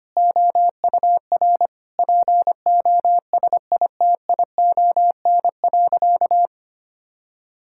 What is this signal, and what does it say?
Morse code, heard as a keyed tone, about 25 wpm, 700 Hz. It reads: OUR POSITION.
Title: Sentences from Top 1000 Words - Review Part 5 - 25wpm